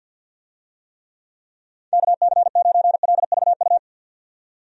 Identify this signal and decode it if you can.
Morse code, heard as a keyed tone, about 47 wpm, 700 Hz. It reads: KC9LVU
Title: USA Call Signs - 6 Characters - 50wpm